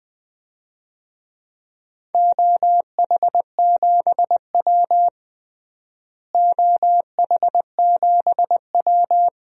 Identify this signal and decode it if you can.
Morse code, heard as a keyed tone, about 20 wpm, 700 Hz. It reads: OH7W OH7W
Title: DXCC-100 Call Signs - 4 Characters - Repeated 2x - 20wpm